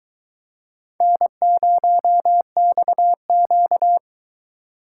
Morse code, heard as a keyed tone, about 23 wpm, 700 Hz. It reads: N0XQ